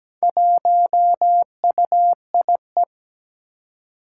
Morse code, heard as a keyed tone, about 17 wpm, 700 Hz. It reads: 1UIE
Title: Four Challenging Letter-Number Combinations 17wpm